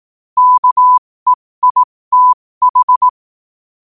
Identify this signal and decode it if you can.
Morse code, heard as a keyed tone, about 18 wpm, spaced as for 15 wpm, 1000 Hz. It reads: KEITH
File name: qotdfiles / 15wpm